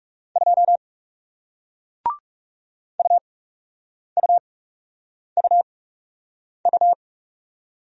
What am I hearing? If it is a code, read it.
Morse code, heard as a keyed tone, about 40 wpm, spaced as for 8 wpm, 700 Hz. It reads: 2 UUUU